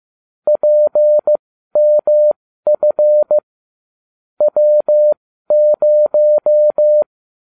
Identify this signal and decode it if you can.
Morse code, heard as a keyed tone, about 15 wpm, spaced as for 12 wpm, 600 Hz. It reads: PMF W0